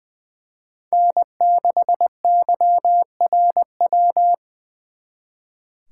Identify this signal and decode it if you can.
Morse code, heard as a keyed tone, about 24 wpm, 700 Hz. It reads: N6YRW
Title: Speed Racing - USA 1x3 Call Signs 20wpm